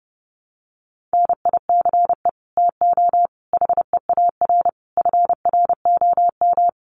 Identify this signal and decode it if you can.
Morse code, heard as a keyed tone, about 30 wpm, 700 Hz. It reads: NICE TO HEAR FROM
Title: Common QSO Phrases 30wpm